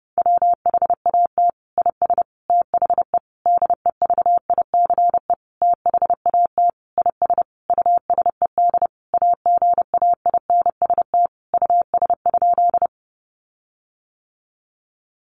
Morse code, heard as a keyed tone, about 28 wpm, 700 Hz. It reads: WHAT IS THE DEVICE THAT IS USED AGAINST US?
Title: Sentences from Top 1500 Words 30wpm